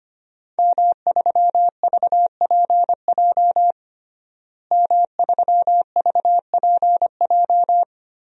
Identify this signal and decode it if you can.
Morse code, heard as a keyed tone, about 25 wpm, 700 Hz. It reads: M3VPJ M3VPJ
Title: DXCC-100 Call Signs - 5 Characters - Repeated 2x - 25wpm